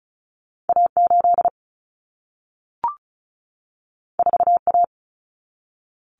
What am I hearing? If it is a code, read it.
Morse code, heard as a keyed tone, about 36 wpm, 700 Hz. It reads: A8 4A
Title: Common Cut Numbers - 2-Digit 35wpm